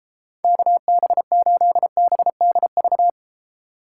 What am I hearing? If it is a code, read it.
Morse code, heard as a keyed tone, about 32 wpm, 700 Hz. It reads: KB8BDV